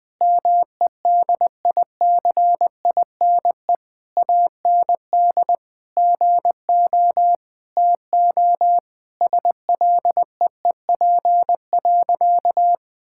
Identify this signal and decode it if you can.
Morse code, heard as a keyed tone, about 20 wpm, 700 Hz. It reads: MEDICINE AND GO TO SLEEP.
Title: Sentences from Top 1000 Words - Review Part 6 - 20wpm